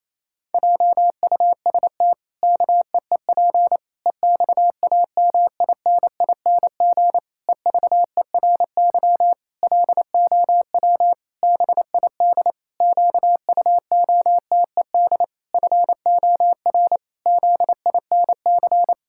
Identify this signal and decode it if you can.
Morse code, heard as a keyed tone, about 28 wpm, 700 Hz. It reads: JUST KEEP EXAMINING EVERY LOW BID QUOTED FOR ZINC
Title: Pangram Challenge v2 - 28wpm